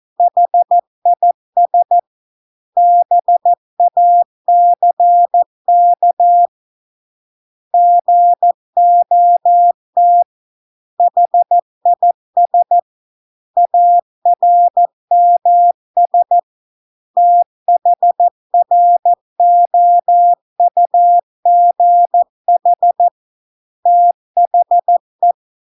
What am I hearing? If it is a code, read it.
Morse code, heard as a keyed tone, about 14 wpm, 700 Hz. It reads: HIS BACK GOT HIS ARMS THROUGH THE